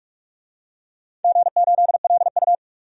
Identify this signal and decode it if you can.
Morse code, heard as a keyed tone, about 46 wpm, 700 Hz. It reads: G8LU
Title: DXCC-100 Call Signs - 4 Characters - 45wpm